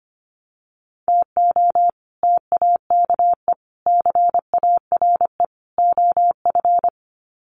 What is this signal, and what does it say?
Morse code, heard as a keyed tone, about 25 wpm, 700 Hz. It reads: TO TAKE CARE OF
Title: Sets of 4 Words 25wpm